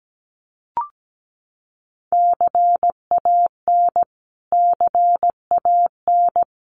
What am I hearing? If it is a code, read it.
Morse code, heard as a keyed tone, about 17 wpm, 700 Hz. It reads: CAN CAN
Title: Top 200 Words Repeated 17wpm